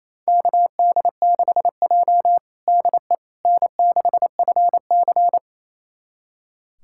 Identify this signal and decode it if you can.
Morse code, heard as a keyed tone, about 28 wpm, 690 Hz. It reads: KD6J DE N6FC
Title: USA Callsign DE Callsign - 4 Characters - 28wpm